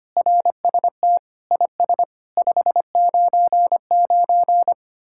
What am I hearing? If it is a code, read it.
Morse code, heard as a keyed tone, about 32 wpm, 700 Hz. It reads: RST IS 599